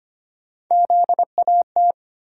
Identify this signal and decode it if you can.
Morse code, heard as a keyed tone, about 25 wpm, 700 Hz. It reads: ZAT
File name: Three Letter-Number Combinations - 25wpm